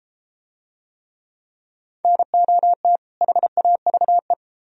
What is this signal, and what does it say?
Morse code, heard as a keyed tone, about 30 wpm, 700 Hz. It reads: NOT HAVE